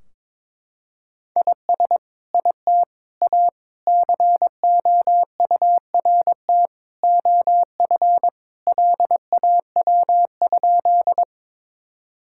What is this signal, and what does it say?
Morse code, heard as a keyed tone, about 22 wpm, 700 Hz. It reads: IS IT A COURT OF LAW?
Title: Sentences from Top 1600 Words 22wpm